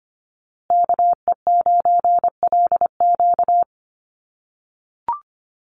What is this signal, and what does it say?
Morse code, heard as a keyed tone, about 25 wpm, 700 Hz. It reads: KE9LQ
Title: USA Call Signs Repeated 3x 25wpm